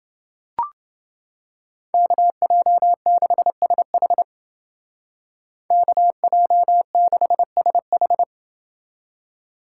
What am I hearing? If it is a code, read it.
Morse code, heard as a keyed tone, about 29 wpm, 700 Hz. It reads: KJ6SH KJ6SH